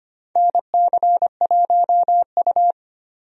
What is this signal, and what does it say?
Morse code, heard as a keyed tone, about 25 wpm, 700 Hz. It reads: NC1U
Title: USA Call Signs - 4 Characters - 25wpm